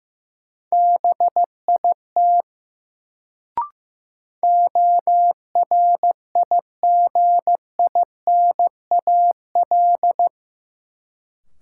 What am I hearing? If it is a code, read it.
Morse code, heard as a keyed tone, about 15 wpm, 700 Hz. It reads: BIT ORIGINAL
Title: Sentences from Top 900 Words - New Word Drill - 15wpm